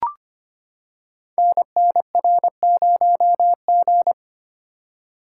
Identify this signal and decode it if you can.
Morse code, heard as a keyed tone, about 25 wpm, 700 Hz. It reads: NNR0G
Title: Code Groups - 5 Letter:Numbers - 25wpm